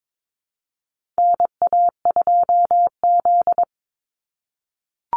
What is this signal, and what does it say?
Morse code, heard as a keyed tone, about 22 wpm, 700 Hz. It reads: NA2Z
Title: USA Call Signs 22wpm